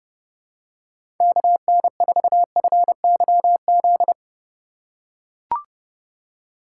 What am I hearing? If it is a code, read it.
Morse code, heard as a keyed tone, about 30 wpm, 700 Hz. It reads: KN4FYZ
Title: USA Call Signs - 6 Characters - 4x Character Spacing - 30wpm